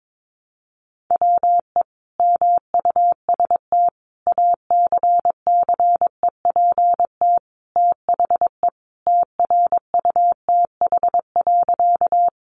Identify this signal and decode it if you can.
Morse code, heard as a keyed tone, about 22 wpm, 700 Hz. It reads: WE MUST ACCEPT THE TRUTH.
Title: Sentences from Top 1300 Words 22wpm